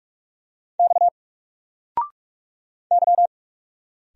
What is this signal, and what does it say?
Morse code, heard as a keyed tone, about 44 wpm, 700 Hz. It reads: X Y
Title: Single Character (Letter:Number:Punctuation:Prosign) 45wpm